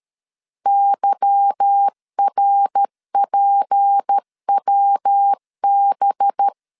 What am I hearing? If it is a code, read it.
Morse code, heard as a keyed tone, about 13 wpm, 800 Hz. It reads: YRPWB
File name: LEZIONI / 8 YJ45 / 8 Y 12wpm 2228